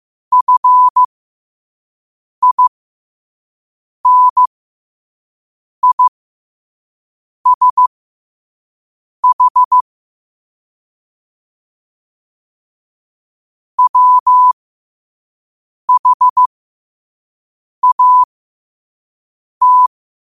Morse code, heard as a keyed tone, about 15 wpm, spaced as for 5 wpm, 1000 Hz. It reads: FINISH WHAT
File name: qotdfiles / 5wpm